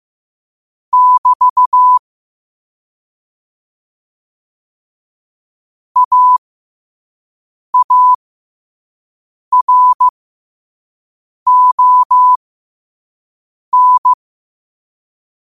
Morse code, heard as a keyed tone, about 15 wpm, spaced as for 5 wpm, 1000 Hz. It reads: <BT> AARON
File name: 5wpm